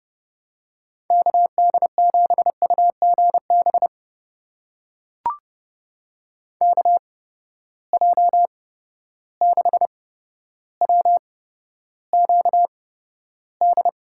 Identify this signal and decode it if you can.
Morse code, heard as a keyed tone, about 30 wpm, spaced as for 13 wpm, 700 Hz. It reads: KD7UGB KJ6WQD
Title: USA Call Signs - 6 Characters - 8x Character Spacing - 30wpm